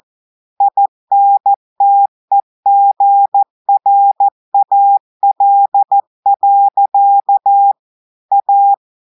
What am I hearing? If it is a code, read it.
Morse code, heard as a keyed tone, about 14 wpm, 800 Hz. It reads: INTEGRAL. A
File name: audios / portugais / portugais texte en clair 1 vitesse 14 mots minutes